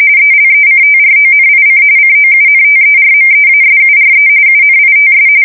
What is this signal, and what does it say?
RTTY, about 45.45 baud. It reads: ELECTRONIC COUNTERMEASURE UNITS
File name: fsk170 / battleshipnj